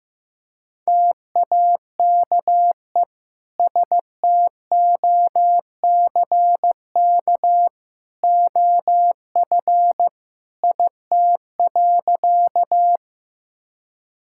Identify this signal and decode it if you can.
Morse code, heard as a keyed tone, about 15 wpm, 700 Hz. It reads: TAKE STOCK OF IT.
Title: Sentences from Top 1000 Words - Review Part 5 - 15wpm